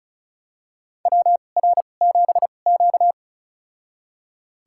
Morse code, heard as a keyed tone, about 34 wpm, spaced as for 25 wpm, 700 Hz. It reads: WR7Q
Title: USA Call Signs - 4 Characters - 2x Character Spacing - 35wpm